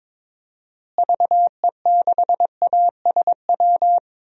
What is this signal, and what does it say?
Morse code, heard as a keyed tone, about 22 wpm, 700 Hz. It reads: VE6ASW
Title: DXCC-100 Call Signs - 6 Characters - Repeated 2x - 22wpm